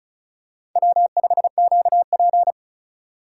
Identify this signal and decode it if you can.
Morse code, heard as a keyed tone, about 34 wpm, 700 Hz. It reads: W5QP
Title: K1USN SST QSO Practice - 35wpm